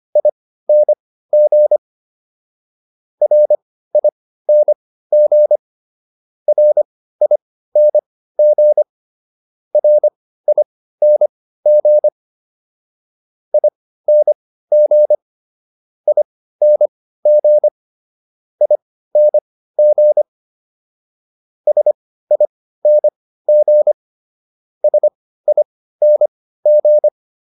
Morse code, heard as a keyed tone, about 25 wpm, spaced as for 15 wpm, 600 Hz. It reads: ING RING RING RING ING ING ING SING SING